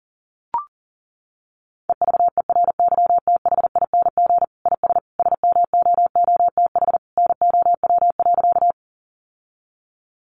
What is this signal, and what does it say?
Morse code, heard as a keyed tone, about 40 wpm, 700 Hz. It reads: EVERYTHING IS SMOOTH NOW.
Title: Sentences from Top 1400 Words 40wpm